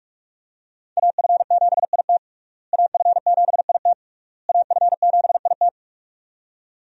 Morse code, heard as a keyed tone, about 46 wpm, 700 Hz. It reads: AF7IT AF7IT AF7IT
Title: USA Call Signs - 5 Characters - Repeated 3x - 45wpm